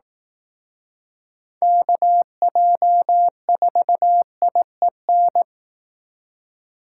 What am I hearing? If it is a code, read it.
Morse code, heard as a keyed tone, about 18 wpm, 700 Hz. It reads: KJ4IEN
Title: Speed Racing - USA 2x3 Call Signs 15wpm